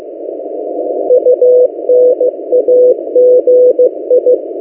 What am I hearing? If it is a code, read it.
Morse code, heard as a keyed tone, about 15 wpm, 500 Hz. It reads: UNAGI
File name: M08a- -1355utc-8096khz